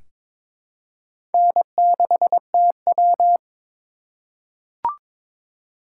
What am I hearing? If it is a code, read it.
Morse code, heard as a keyed tone, about 22 wpm, 700 Hz. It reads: N6TW
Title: K1USN SST QSO Practice - 22wpm